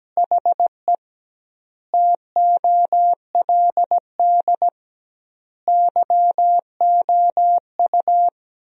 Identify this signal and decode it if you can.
Morse code, heard as a keyed tone, about 17 wpm, 700 Hz. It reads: HE TOLD YOU
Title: 2x Word-Spacing - Sentences from Top 500 Words - 17wpm